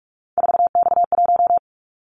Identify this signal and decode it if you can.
Morse code, heard as a keyed tone, about 46 wpm, 700 Hz. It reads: 4X1
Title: Callsign Prefixes for Top 50 DXCC - 45wpm